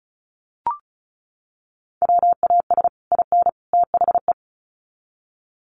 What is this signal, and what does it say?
Morse code, heard as a keyed tone, about 34 wpm, 700 Hz. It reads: WAS IN THE
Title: Sets of 3 Words 35wpm